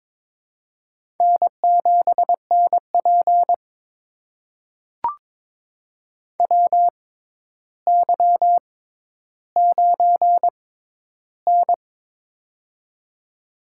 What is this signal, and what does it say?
Morse code, heard as a keyed tone, about 22 wpm, spaced as for 11 wpm, 700 Hz. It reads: N7NP WY9N
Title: USA Call Signs - 4 Characters - 6x Character Spacing - 22wpm